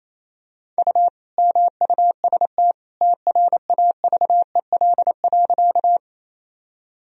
Morse code, heard as a keyed tone, about 28 wpm, 700 Hz. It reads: U MUST TRAVEL.
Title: Sentences from Top 400 Words and Abbreviations 28wpm